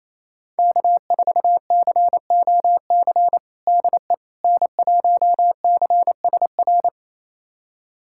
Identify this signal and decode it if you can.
Morse code, heard as a keyed tone, about 28 wpm, 700 Hz. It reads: K4COC DE N1CSR
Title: USA Callsign DE Callsign 28wpm